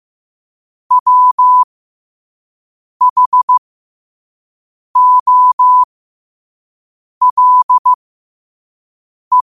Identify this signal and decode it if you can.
Morse code, heard as a keyed tone, about 15 wpm, spaced as for 5 wpm, 1000 Hz. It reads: WHOLE
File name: qotdfiles / 5wpm